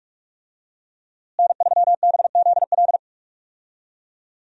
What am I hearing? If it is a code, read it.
Morse code, heard as a keyed tone, about 41 wpm, 700 Hz. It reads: N3BZL
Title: USA Call Signs - 5 Characters - Repeated 3x - 45wpm